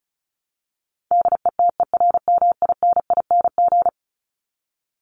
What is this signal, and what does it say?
Morse code, heard as a keyed tone, about 34 wpm, 700 Hz. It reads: DETERMINING
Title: ING Suffix - Top 100 Words with Base Words 35wpm